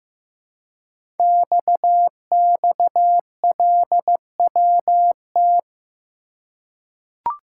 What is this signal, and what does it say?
Morse code, heard as a keyed tone, about 15 wpm, 700 Hz. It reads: XXLWT
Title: Code Groups - 5 Letters - 15wpm